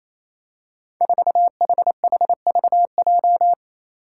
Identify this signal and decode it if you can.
Morse code, heard as a keyed tone, about 27 wpm, 700 Hz. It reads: 4HHVJ
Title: Five Challenging Letter-Number Combinations 28wpm